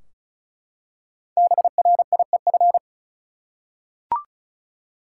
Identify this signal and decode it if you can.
Morse code, heard as a keyed tone, about 36 wpm, 700 Hz. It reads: BRIEF
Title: Five Letter Words - 35wpm